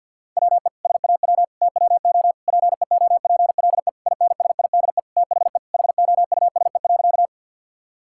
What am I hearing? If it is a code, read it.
Morse code, heard as a keyed tone, about 50 wpm, 700 Hz. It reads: WE SAW TWO PEOPLE INSIDE THE HOUSE.
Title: Sentences from Top 1000 Words - Review Part 6 - 50wpm